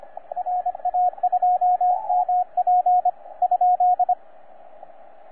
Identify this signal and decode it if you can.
Morse code, heard as a keyed tone, about 25 wpm, 720 Hz, with radio noise. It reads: RA2MP ?
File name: J6/N7QT 20CW